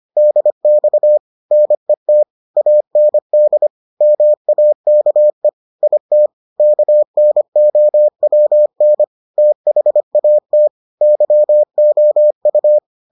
Morse code, heard as a keyed tone, about 25 wpm, 600 Hz. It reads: DX NET AND MAKE IT KNOWN THAT YOU